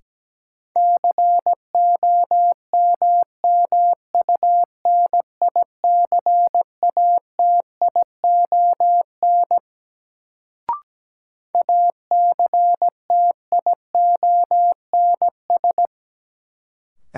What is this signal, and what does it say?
Morse code, heard as a keyed tone, about 17 wpm, 700 Hz. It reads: COMMUNICATION ACTIONS
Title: Sentences from Top 1000 Words - Review Part 6 - New Word Drill - 17wpm